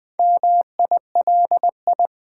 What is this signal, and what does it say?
Morse code, heard as a keyed tone, about 20 wpm, 700 Hz. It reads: MILI
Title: Build A Word - 2 Letters At A Time - 8-Letter Words - 20wpm